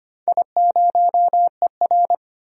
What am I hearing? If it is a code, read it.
Morse code, heard as a keyed tone, about 25 wpm, 700 Hz. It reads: I0ER
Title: DXCC-100 Call Signs - 4 Characters - 25wpm